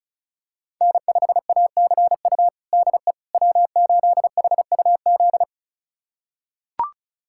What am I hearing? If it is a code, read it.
Morse code, heard as a keyed tone, about 34 wpm, 700 Hz. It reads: N5ACU DE W8HUZ
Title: USA Callsign DE Callsign - 5 Characters - 35wpm